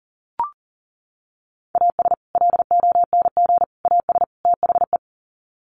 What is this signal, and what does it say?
Morse code, heard as a keyed tone, about 40 wpm, 700 Hz. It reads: AS LONG AS THE
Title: Sets of 4 Words 40wpm